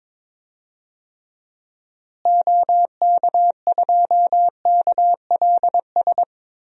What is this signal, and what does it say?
Morse code, heard as a keyed tone, about 22 wpm, 700 Hz. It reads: OK2KLS
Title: DXCC-100 Call Signs - 6 Characters - Repeated 3x - 22wpm